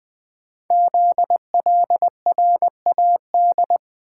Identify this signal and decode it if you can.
Morse code, heard as a keyed tone, about 20 wpm, 700 Hz. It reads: ZLRAD
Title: Code Groups - 5 Letter:Numbers - 20wpm